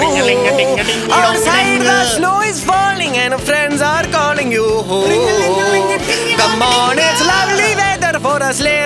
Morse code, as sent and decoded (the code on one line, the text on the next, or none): none